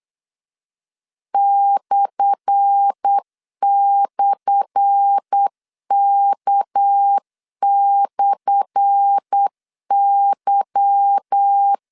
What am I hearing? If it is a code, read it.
Morse code, heard as a keyed tone, about 8 wpm, 800 Hz. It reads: //K/Y
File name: LEZIONI / 10 SIGNS / 10A SLASH 8wpm 3452